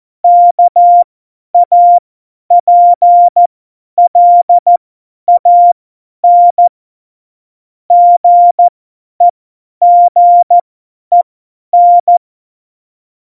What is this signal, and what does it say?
Morse code, heard as a keyed tone, about 14 wpm, spaced as for 10 wpm, 700 Hz. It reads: KAPLAN GEGEN
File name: Morsekurs / Witze / Witz 01. Dezember 2020, 50 F70